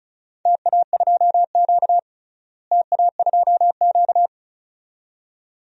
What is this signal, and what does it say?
Morse code, heard as a keyed tone, about 36 wpm, 700 Hz. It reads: TA2Q TA2Q
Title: DXCC-100 Call Signs - 4 Characters - Repeated 2x - 35wpm